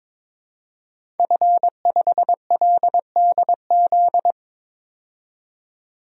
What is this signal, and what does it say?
Morse code, heard as a keyed tone, about 21 wpm, 700 Hz. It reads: F5LDZ
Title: DXCC-100 Call Signs - 5 Characters - 22wpm